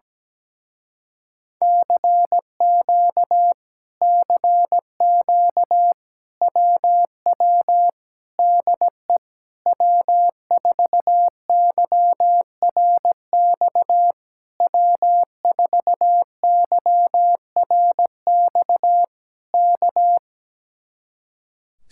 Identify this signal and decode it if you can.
Morse code, heard as a keyed tone, about 17 wpm, 700 Hz. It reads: CQ CQ WW DE W4YRX W4YRX K
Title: Speed-Racing Common QSO Phrases - 17wpm